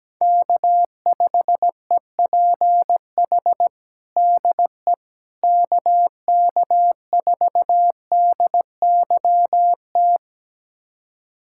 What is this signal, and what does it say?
Morse code, heard as a keyed tone, about 17 wpm, 700 Hz. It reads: K5EPH DE KK4DYT
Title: USA Callsign DE Callsign 17wpm